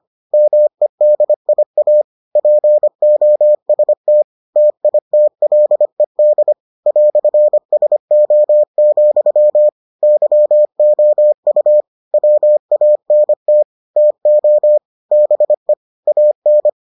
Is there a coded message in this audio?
Yes: Morse code from 0.3 to 16.7 s, heard as a keyed tone, about 25 wpm, 600 Hz. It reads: MEDIA POST TITLED "SO, YOU WANT TO BE AN